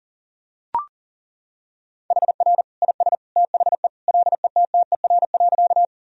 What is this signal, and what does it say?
Morse code, heard as a keyed tone, about 37 wpm, 700 Hz. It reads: HR IS THE LETTER.
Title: Sentences from Top 300 Words and Abbreviations 40wpm